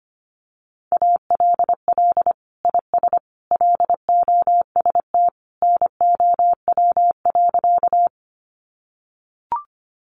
Morse code, heard as a keyed tone, about 25 wpm, 700 Hz. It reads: ALL IS LOST NOW.
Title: Sentences from Top 1400 Words 25wpm